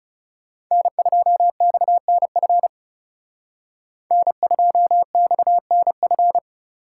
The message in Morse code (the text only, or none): N2XNF N2XNF